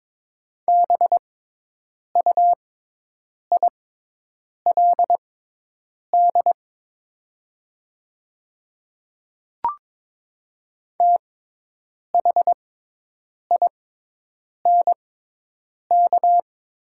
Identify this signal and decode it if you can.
Morse code, heard as a keyed tone, about 22 wpm, spaced as for 8 wpm, 700 Hz. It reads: BUILD THINK
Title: 6x Character Spacing - Top 300 Words - 22wpm